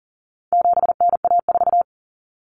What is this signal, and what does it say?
Morse code, heard as a keyed tone, about 40 wpm, 700 Hz. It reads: 7NA4
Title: Common Cut Numbers - 4-Digit 40wpm